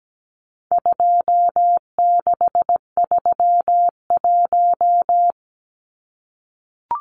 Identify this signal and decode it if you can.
Morse code, heard as a keyed tone, about 17 wpm, 700 Hz. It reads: 2631
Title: Numbers - 4-Digit 17wpm